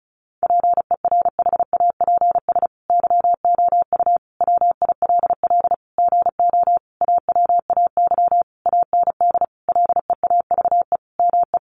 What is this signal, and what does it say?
Morse code, heard as a keyed tone, about 34 wpm, 700 Hz. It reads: PERHAPS YOU WILL GO AWAY AND LEAVE ME
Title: Sentences from Top 1300 Words 35wpm